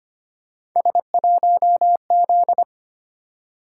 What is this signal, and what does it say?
Morse code, heard as a keyed tone, about 25 wpm, 700 Hz. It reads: S1Z
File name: Three Letter-Number Combinations - 25wpm